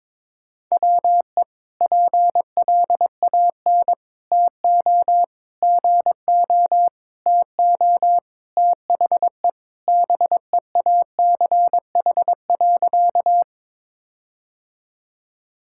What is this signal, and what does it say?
Morse code, heard as a keyed tone, about 22 wpm, 700 Hz. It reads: WE PLAN TO GO TO THE BEACH.
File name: practice / 22wpm / Sentences with 4-Word Phrases - 22wpm